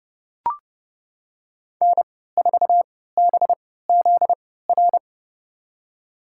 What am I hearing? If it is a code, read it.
Morse code, heard as a keyed tone, about 30 wpm, spaced as for 17 wpm, 700 Hz. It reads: N4BZR
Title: USA Call Signs - 3x Character Spacing - 30wpm